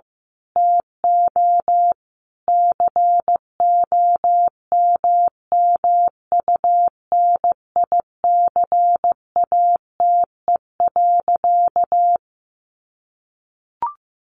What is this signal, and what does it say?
Morse code, heard as a keyed tone, about 15 wpm, 700 Hz. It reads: TO COMMUNICATE.
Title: Sentences from Top 1300 Words 15wpm